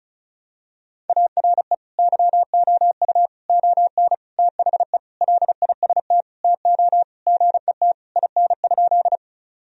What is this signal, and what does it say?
Morse code, heard as a keyed tone, about 36 wpm, 700 Hz. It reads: ARE YOU ON THE LIST TO GET IN?
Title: Sentences from Top 1000 Words - Review Part 2 - 35wpm